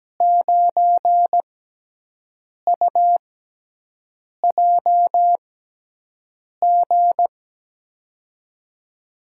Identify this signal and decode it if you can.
Morse code, heard as a keyed tone, about 17 wpm, 700 Hz. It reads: 9 U J G
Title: USA Call Signs - 6 Characters - 6x Character Spacing - 17wpm